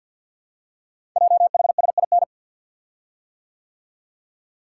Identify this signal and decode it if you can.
Morse code, heard as a keyed tone, about 50 wpm, 690 Hz. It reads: JHSIN
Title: Code Groups - 5 Letters - 50wpm